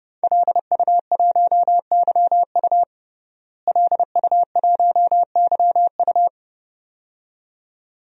Morse code, heard as a keyed tone, about 32 wpm, 700 Hz. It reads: LU1YU LU1YU
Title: DXCC-100 Call Signs - 5 Characters - Repeated 2x - 30wpm